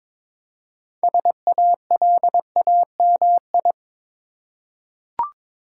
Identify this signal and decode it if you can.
Morse code, heard as a keyed tone, about 22 wpm, 690 Hz. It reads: SALAMI
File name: Difficult Words with Hints - 22wpm